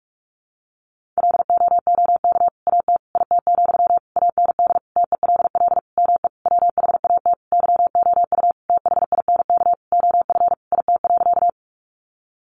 Morse code, heard as a keyed tone, about 44 wpm, 700 Hz. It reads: LOOK AT IT, AND TELL ME WHAT YOU THINK OF IT.
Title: Sentences from Top 200 Words 45wpm